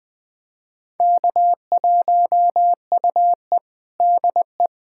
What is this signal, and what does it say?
Morse code, heard as a keyed tone, about 20 wpm, 700 Hz. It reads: K1UE DE
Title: USA Callsign DE Callsign - 4 Characters - 20wpm